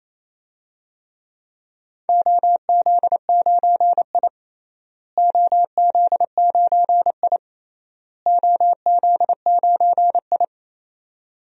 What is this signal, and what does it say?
Morse code, heard as a keyed tone, about 28 wpm, 700 Hz. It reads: OZ9I OZ9I OZ9I
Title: DXCC-100 Call Signs - 4 Characters - Repeated 3x - 28wpm